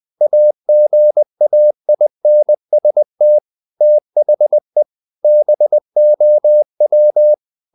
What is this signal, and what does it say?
Morse code, heard as a keyed tone, about 20 wpm, 600 Hz. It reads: AGAINST THE BOW
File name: the old man and the sea part 2 20wpm